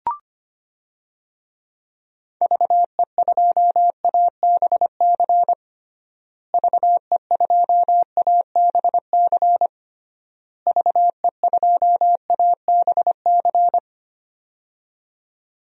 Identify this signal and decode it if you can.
Morse code, heard as a keyed tone, about 25 wpm, 700 Hz. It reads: VE2ABC VE2ABC VE2ABC
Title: DXCC-100 Call Signs - 6 Characters - Repeated 3x - 25wpm